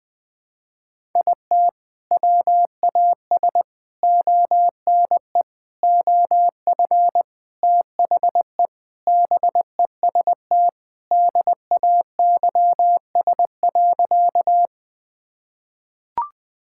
Morse code, heard as a keyed tone, about 20 wpm, 700 Hz. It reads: IT WAS ONE OF THE BEST DAYS.
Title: Sentences with 3-Word Phrases - 20wpm